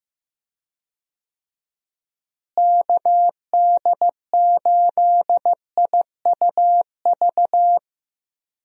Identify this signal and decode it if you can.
Morse code, heard as a keyed tone, about 15 wpm, 700 Hz. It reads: KD8IUV
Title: USA Call Signs - 6 Characters - Repeated 2x - 15wpm